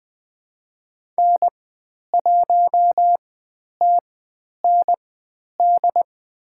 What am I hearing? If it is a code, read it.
Morse code, heard as a keyed tone, about 20 wpm, spaced as for 10 wpm, 700 Hz. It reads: N1TND
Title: 102 Lesson 17g - Letter u - Call Signs 20wpm